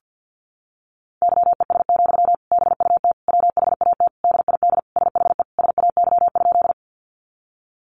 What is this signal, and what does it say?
Morse code, heard as a keyed tone, about 50 wpm, 700 Hz. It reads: YES, BUT WHAT DID SHE SAY?
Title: Sentences from Top 500 Words 50wpm